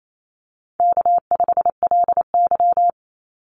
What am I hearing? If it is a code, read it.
Morse code, heard as a keyed tone, about 28 wpm, 700 Hz. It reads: K5LY
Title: Call Signs Repeated 2x - 28wpm